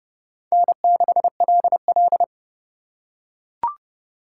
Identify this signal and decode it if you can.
Morse code, heard as a keyed tone, about 32 wpm, 700 Hz. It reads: N6LL